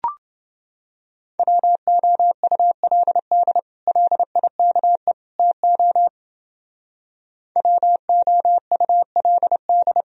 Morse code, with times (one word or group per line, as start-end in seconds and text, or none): 1.4-3.6 WOULD
3.9-5.1 LIKE
5.4-6.1 TO
7.6-10.0 WOULD